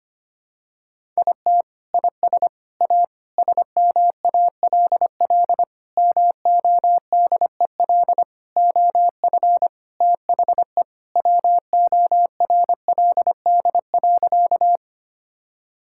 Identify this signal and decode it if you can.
Morse code, heard as a keyed tone, about 25 wpm, 700 Hz. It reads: IT IS A SMALL MODEL OF THE WORLD.
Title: Sentences from Top 1000 Words - Review Part 5 - 25wpm